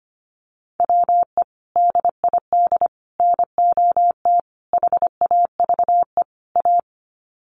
Morse code, heard as a keyed tone, about 25 wpm, 700 Hz. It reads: WE DID NOT HAVE A
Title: Sets of 5 Words Encore2 25wpm